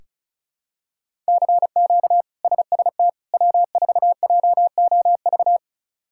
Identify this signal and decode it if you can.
Morse code, heard as a keyed tone, about 36 wpm, 700 Hz. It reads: CQ SST W4JOV
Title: K1USN SST QSO Practice - 35wpm